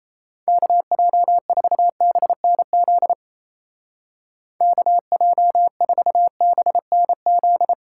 Morse code, heard as a keyed tone, about 37 wpm, 700 Hz. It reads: KJ4BNZ KJ4BNZ